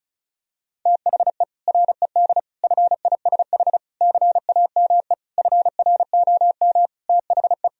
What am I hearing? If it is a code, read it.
Morse code, heard as a keyed tone, about 34 wpm, 700 Hz. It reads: THE RED FISH CAME FROM THE
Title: Sentences from Top 1000 Words - Review Part 2 - 35wpm